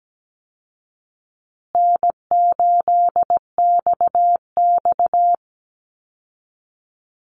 Morse code, heard as a keyed tone, about 17 wpm, 700 Hz. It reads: N8XX
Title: Call Signs Repeated 2x - 17wpm